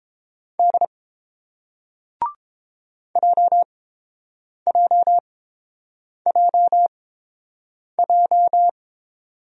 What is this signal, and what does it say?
Morse code, heard as a keyed tone, about 30 wpm, spaced as for 7 wpm, 700 Hz. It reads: D JJJJ